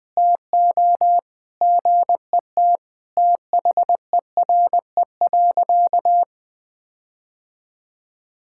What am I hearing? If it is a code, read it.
Morse code, heard as a keyed tone, about 21 wpm, 700 Hz. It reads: TO GET THERE.
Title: Sentences with 5-Word Phrases - 20wpm